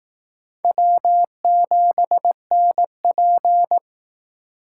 Morse code, heard as a keyed tone, about 18 wpm, 700 Hz. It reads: W7NP
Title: Speed Racing - USA 1x2 Call Signs 15wpm